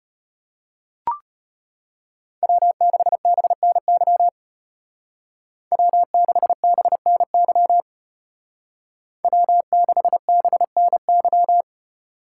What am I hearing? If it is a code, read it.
Morse code, heard as a keyed tone, about 33 wpm, 700 Hz. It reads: W6BNY W6BNY W6BNY